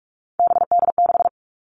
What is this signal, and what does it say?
Morse code, heard as a keyed tone, about 44 wpm, 700 Hz. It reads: BD6
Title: All Cut Numbers - 3-Digit 45wpm